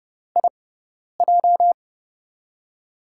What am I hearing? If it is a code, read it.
Morse code, heard as a keyed tone, about 30 wpm, spaced as for 10 wpm, 700 Hz. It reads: IJ
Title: USA Call Signs - 4 Characters - 6x Character Spacing - 30wpm